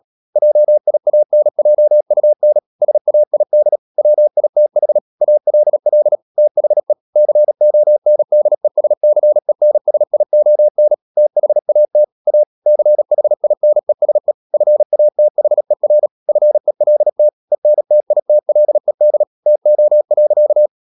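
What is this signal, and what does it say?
Morse code, heard as a keyed tone, about 37 wpm, 600 Hz. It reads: JIANJUN SAID WITH ALL THE CONDESCENSION THAT A CHINESE FATHER FELT ENTITLED TO.
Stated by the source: Morse code practice at 35–39 wpm sending English text of Ken Follett's novel Never, from CW med Gnister.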